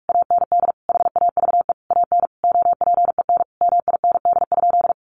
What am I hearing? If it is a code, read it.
Morse code, heard as a keyed tone, about 44 wpm, 700 Hz. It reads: AND HAVE AN OPEN MIND?
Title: Sentences from Top 1300 Words 45wpm